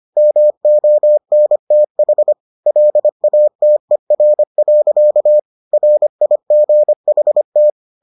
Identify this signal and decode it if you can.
Morse code, heard as a keyed tone, about 25 wpm, 600 Hz. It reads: MONTH LATER. RIGHT